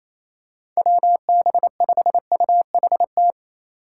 Morse code, heard as a keyed tone, about 28 wpm, 700 Hz. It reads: WB5UHT
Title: USA Call Signs - 6 Characters - 4x Character Spacing - 28wpm